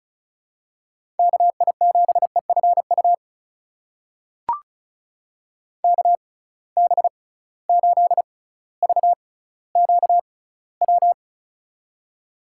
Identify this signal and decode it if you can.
Morse code, heard as a keyed tone, about 36 wpm, spaced as for 18 wpm, 700 Hz. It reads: KI7EFU KB8VQW
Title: USA Call Signs - 6 Characters - 6x Character Spacing - 35wpm